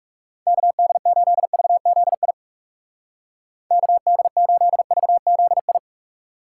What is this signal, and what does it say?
Morse code, heard as a keyed tone, about 38 wpm, 700 Hz. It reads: KD8VZI KD8VZI